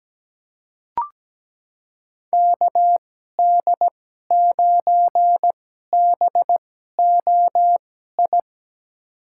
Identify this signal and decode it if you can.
Morse code, heard as a keyed tone, about 17 wpm, spaced as for 12 wpm, 700 Hz. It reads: KD9BOI
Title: USA Call Signs - 6 Characters - 2x Character Spacing - 17wpm